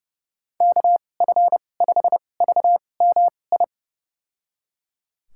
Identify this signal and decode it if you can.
Morse code, heard as a keyed tone, about 30 wpm, spaced as for 22 wpm, 700 Hz. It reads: KF5VMI
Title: USA Call Signs - 6 Characters - 2x Character Spacing - 30wpm